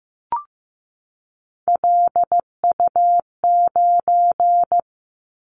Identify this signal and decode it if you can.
Morse code, heard as a keyed tone, about 15 wpm, 700 Hz. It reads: LU9
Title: Callsign Prefixes for Top 100 DXCC - 15wpm